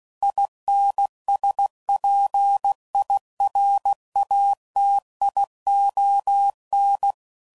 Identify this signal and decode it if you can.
Morse code, heard as a keyed tone, about 16 wpm, 800 Hz. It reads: INSPIRATION
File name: Sayings / Quotes / Persperation